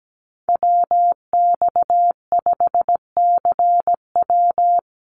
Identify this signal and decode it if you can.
Morse code, heard as a keyed tone, about 17 wpm, 700 Hz. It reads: WX5CW
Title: Call Signs Repeated 3x - 17wpm